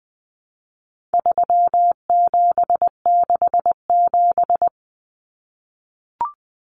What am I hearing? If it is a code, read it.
Morse code, heard as a keyed tone, about 20 wpm, 700 Hz. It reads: 3767
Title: Numbers - 4-Digit 20wpm